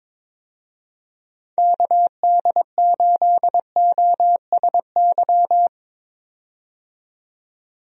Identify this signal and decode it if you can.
Morse code, heard as a keyed tone, about 22 wpm, 700 Hz. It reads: KD8OSY
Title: USA Call Signs - 6 Characters - 22wpm